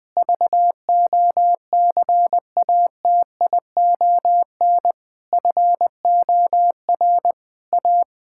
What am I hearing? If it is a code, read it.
Morse code, heard as a keyed tone, about 20 wpm, 700 Hz. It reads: VOCATION FOR A
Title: Pangram Challenge v2 - 20wpm